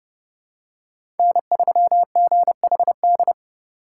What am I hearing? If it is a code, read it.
Morse code, heard as a keyed tone, about 31 wpm, 700 Hz. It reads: N3GHD
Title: USA Call Signs - 5 Characters - 3x Character Spacing - 30wpm